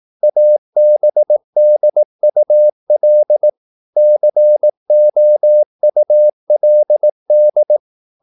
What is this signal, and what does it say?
Morse code, heard as a keyed tone, about 18 wpm, 600 Hz. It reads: ABDUL COULD
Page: Romanen Never af Ken Follet på engelsk.